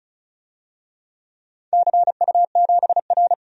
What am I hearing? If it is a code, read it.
Morse code, heard as a keyed tone, about 36 wpm, 700 Hz. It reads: CU7R
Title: DXCC-100 Call Signs - 4 Characters - 35wpm